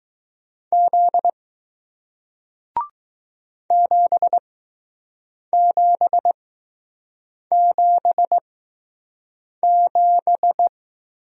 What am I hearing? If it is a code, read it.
Morse code, heard as a keyed tone, about 20 wpm, spaced as for 6 wpm, 700 Hz. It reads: Z 7777